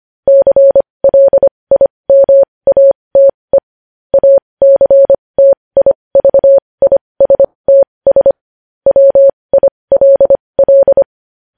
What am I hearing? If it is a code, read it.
Morse code, heard as a keyed tone, about 25 wpm, spaced as for 20 wpm, 550 Hz. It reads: CLIMATE ACTIVISTS WILL
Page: news-headlines-in-morse-code-at-20-wpm